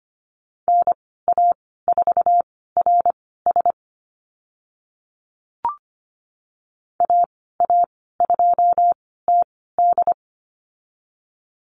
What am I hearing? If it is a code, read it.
Morse code, heard as a keyed tone, about 25 wpm, spaced as for 16 wpm, 700 Hz. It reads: NA4RS AA2TD
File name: Lesson 15g - Number 5 - Call Signs 25wpm